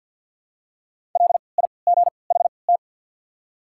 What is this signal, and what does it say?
Morse code, heard as a keyed tone, about 53 wpm, spaced as for 29 wpm, 690 Hz. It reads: LIGHT
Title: 3x Character Spacing - Top 200 Words - 50wpm